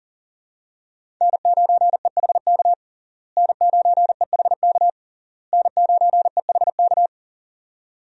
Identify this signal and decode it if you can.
Morse code, heard as a keyed tone, about 40 wpm, 700 Hz. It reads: N9EHK N9EHK N9EHK
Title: USA Call Signs - 5 Characters - Repeated 3x - 40wpm